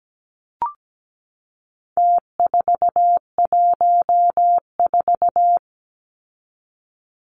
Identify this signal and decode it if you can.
Morse code, heard as a keyed tone, about 17 wpm, 700 Hz. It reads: T414
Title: Numbers - Contest Style - 4-Digit 17wpm